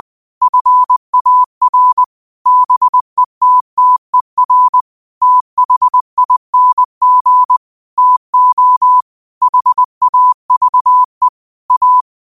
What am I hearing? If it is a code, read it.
Morse code, heard as a keyed tone, about 20 wpm, 1000 Hz. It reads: FAR BETTER THING TO HAVE A
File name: qotdfiles / 20wpm